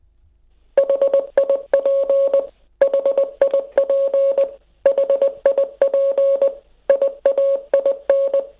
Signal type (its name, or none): Morse code